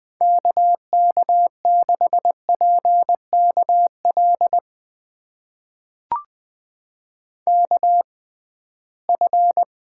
Morse code, heard as a keyed tone, about 21 wpm, spaced as for 15 wpm, 700 Hz. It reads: KK6PKL KF
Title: USA Call Signs - 6 Characters - 6x Character Spacing - 20wpm